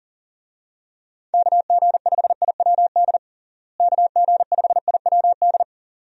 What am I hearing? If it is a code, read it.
Morse code, heard as a keyed tone, about 40 wpm, 700 Hz. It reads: KG5IWD KG5IWD
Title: USA Call Signs - 6 Characters - Repeated 2x - 40wpm